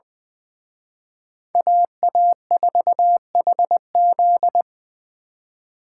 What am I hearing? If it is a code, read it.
Morse code, heard as a keyed tone, about 24 wpm, 700 Hz. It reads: AA4HZ
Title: Speed Racing - USA 2x2 Call Signs 20wpm